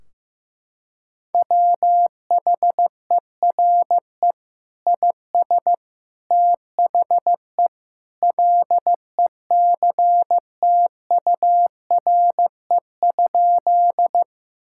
Morse code, heard as a keyed tone, about 15 wpm, 700 Hz. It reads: WHERE IS THE LECTURE?
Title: Sentences from Top 1600 Words 15wpm